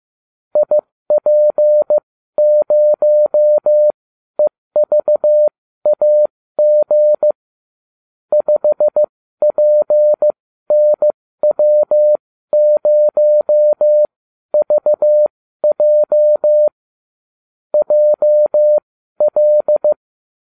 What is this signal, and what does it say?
Morse code, heard as a keyed tone, about 15 wpm, spaced as for 12 wpm, 600 Hz. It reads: IP0EVAG 5PNW0VJ JL